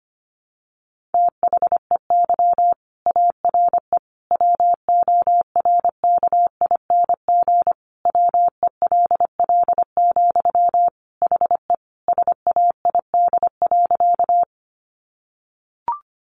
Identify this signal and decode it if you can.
Morse code, heard as a keyed tone, about 25 wpm, 700 Hz. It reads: THEY ARE WORKING WELL, HE SAID.
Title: Sentences from Top 700 Words 25wpm